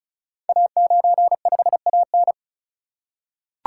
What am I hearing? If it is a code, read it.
Morse code, heard as a keyed tone, about 36 wpm, 700 Hz. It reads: A95AN
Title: Code Groups - 5 Letter:Numbers - 35wpm